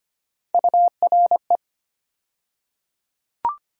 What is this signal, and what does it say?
Morse code, heard as a keyed tone, about 25 wpm, 700 Hz. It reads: URE
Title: USA Call Sign Postfixes - 3 Letters 25wpm